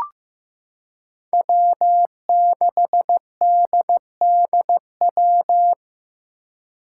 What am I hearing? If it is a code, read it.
Morse code, heard as a keyed tone, about 15 wpm, 700 Hz. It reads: W6DDW
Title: Five Challenging Letter-Number Combinations 15wpm